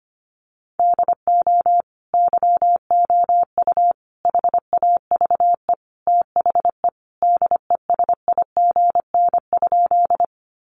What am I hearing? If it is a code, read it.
Morse code, heard as a keyed tone, about 25 wpm, 700 Hz. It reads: DO YOU HAVE THE DESIGN?
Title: Sentences from Top 800 Words 25wpm